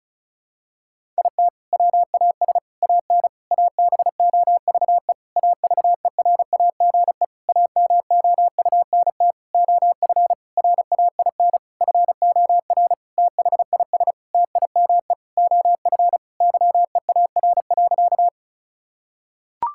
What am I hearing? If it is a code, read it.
Morse code, heard as a keyed tone, about 36 wpm, 700 Hz. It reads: IT WAS AN ABOVE AVERAGE AMOUNT OF RAIN FOR THIS TIME OF YEAR.
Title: Sentences from Top 1000 Words - Review Part 6 - 35wpm